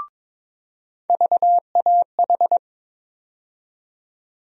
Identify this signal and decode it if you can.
Morse code, heard as a keyed tone, about 22 wpm, 700 Hz. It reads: VAH
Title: USA Call Sign Postfixes - 3 Letters 22wpm